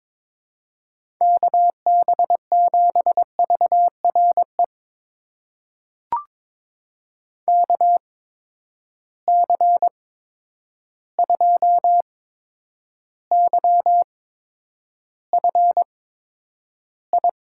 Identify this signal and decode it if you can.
Morse code, heard as a keyed tone, about 22 wpm, spaced as for 9 wpm, 700 Hz. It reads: KB7VRE KC2YFI
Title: USA Call Signs - 6 Characters - 8x Character Spacing - 22wpm